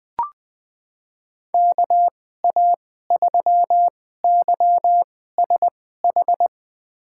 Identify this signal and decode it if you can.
Morse code, heard as a keyed tone, about 20 wpm, spaced as for 14 wpm, 700 Hz. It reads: KA3YSH
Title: USA Call Signs - 2x Character Spacing - 20wpm